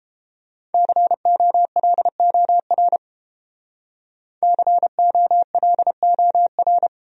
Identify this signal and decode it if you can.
Morse code, heard as a keyed tone, about 31 wpm, 700 Hz. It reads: COLOR COLOR